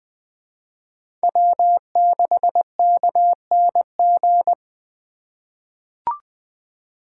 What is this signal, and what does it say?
Morse code, heard as a keyed tone, about 20 wpm, 700 Hz. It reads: W6KNG
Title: USA Call Signs - 3x Character Spacing - 20wpm